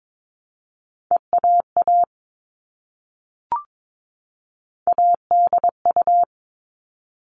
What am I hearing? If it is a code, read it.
Morse code, heard as a keyed tone, about 22 wpm, 700 Hz. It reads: EAA ADU
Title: All Cut Numbers - 3-Digit 22wpm